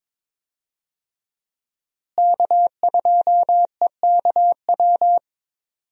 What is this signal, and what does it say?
Morse code, heard as a keyed tone, about 22 wpm, 700 Hz. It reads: K2EKW
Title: USA Call Signs - 5 Characters - Repeated 3x - 22wpm